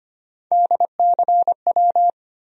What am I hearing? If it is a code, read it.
Morse code, heard as a keyed tone, about 25 wpm, 700 Hz. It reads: DCW
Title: USA Call Sign Postfixes - 3 Letters 25wpm